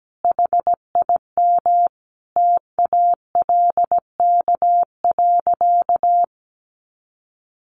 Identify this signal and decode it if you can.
Morse code, heard as a keyed tone, about 17 wpm, 700 Hz. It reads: HIM TALK.
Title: Sentences from Top 600 Words 17wpm